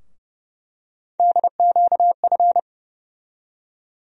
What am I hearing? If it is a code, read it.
Morse code, heard as a keyed tone, about 30 wpm, 700 Hz. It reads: DQF
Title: USA Call Sign Postfixes - 3 Letters 30wpm